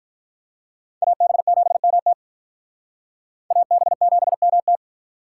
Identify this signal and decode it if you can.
Morse code, heard as a keyed tone, about 42 wpm, 700 Hz. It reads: AB7MT AB7MT
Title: Speed Racing - USA Call Signs 35wpm